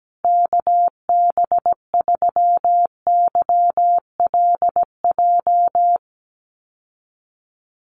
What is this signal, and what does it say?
Morse code, heard as a keyed tone, about 17 wpm, 700 Hz. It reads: KB3YLJ
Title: USA Call Signs Repeated 3x 17wpm